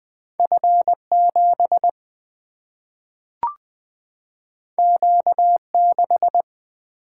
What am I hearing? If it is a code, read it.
Morse code, heard as a keyed tone, about 20 wpm, 700 Hz. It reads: F 7 Q 6
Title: Single-Letter Single-Number 20wpm